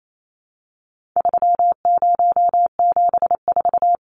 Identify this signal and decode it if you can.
Morse code, heard as a keyed tone, about 28 wpm, 700 Hz. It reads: 3074
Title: Numbers - 4-Digit 28wpm